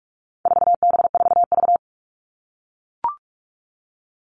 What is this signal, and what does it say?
Morse code, heard as a keyed tone, about 44 wpm, 700 Hz. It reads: 4B4V
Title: All Cut Numbers - 4-Digit 45wpm